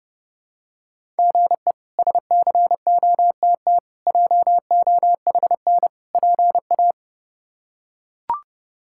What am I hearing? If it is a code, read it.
Morse code, heard as a keyed tone, about 32 wpm, 700 Hz. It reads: GE SCOTT JOHN PA
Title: K1USN SST QSO Practice - 30wpm